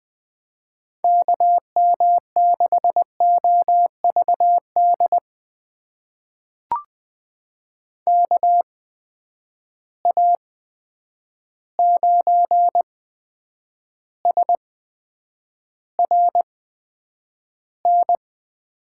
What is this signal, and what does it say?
Morse code, heard as a keyed tone, about 20 wpm, spaced as for 9 wpm, 700 Hz. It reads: KM6OVD KA9SRN
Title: USA Call Signs - 6 Characters - 8x Character Spacing - 20wpm